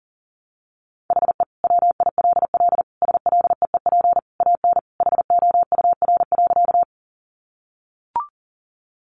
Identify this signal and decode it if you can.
Morse code, heard as a keyed tone, about 40 wpm, 700 Hz. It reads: HE WILL SLEEP AN HOUR.
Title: Sentences from Top 500 Words 40wpm